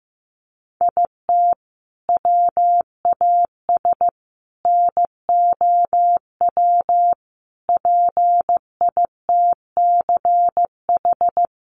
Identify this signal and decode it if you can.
Morse code, heard as a keyed tone, about 15 wpm, 700 Hz. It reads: IT WAS NOW PITCH
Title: Sentences from Top 1300 Words 15wpm